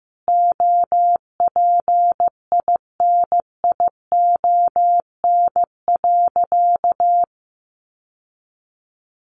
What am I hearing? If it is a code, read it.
Morse code, heard as a keyed tone, about 15 wpm, 700 Hz. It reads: OPINION.
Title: Sentences from Top 1200 Words 15wpm